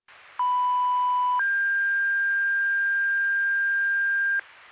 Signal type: two-tone page